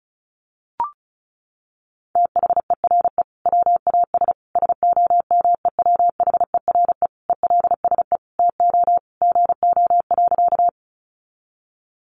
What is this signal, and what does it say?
Morse code, heard as a keyed tone, about 34 wpm, 700 Hz. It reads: THERE WAS SOMEWHERE ELSE TO GO.
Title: Sentences from Top 1400 Words 35wpm